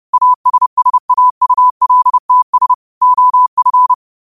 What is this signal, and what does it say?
Morse code, heard as a keyed tone, about 30 wpm, 1000 Hz. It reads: ASSAULTS OF
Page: Quotes for Sun, 17 Aug 2025 in Morse Code at 30 words per minute.